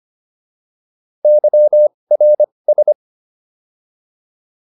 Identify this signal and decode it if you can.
Morse code, heard as a keyed tone, about 25 wpm, spaced as for 20 wpm, 600 Hz. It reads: YRS